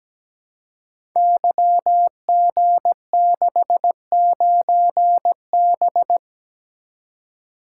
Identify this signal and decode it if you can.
Morse code, heard as a keyed tone, about 17 wpm, 700 Hz. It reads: YG69B
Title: Code Groups - 5 Letter:Numbers - 17wpm